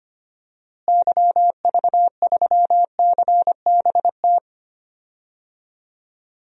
Morse code, heard as a keyed tone, about 25 wpm, 700 Hz. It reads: YV3CBT
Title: DXCC-100 Call Signs - 6 Characters - Repeated 3x - 25wpm